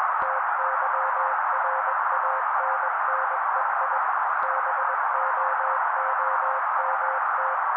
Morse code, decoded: TYRANNEI BOOMT